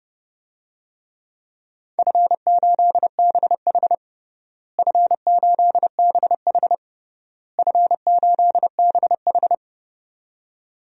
Morse code, heard as a keyed tone, about 30 wpm, 700 Hz. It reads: F8BH F8BH F8BH
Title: DXCC-100 Call Signs - 4 Characters - Repeated 3x - 30wpm